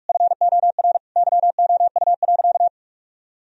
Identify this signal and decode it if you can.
Morse code, heard as a keyed tone, about 41 wpm, 700 Hz. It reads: FOR YOU.
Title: Sentences from Top 1000 Words - Review Part 2 - 45wpm